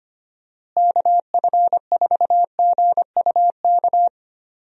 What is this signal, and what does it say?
Morse code, heard as a keyed tone, about 25 wpm, 700 Hz. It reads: KF4GUK
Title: USA Call Signs - 6 Characters - Repeated 2x - 25wpm